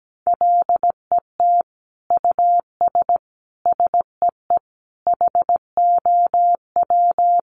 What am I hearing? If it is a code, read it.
Morse code, heard as a keyed tone, about 17 wpm, 700 Hz. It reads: LET US SEE HOW
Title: Sentences from Top 1400 Words 17wpm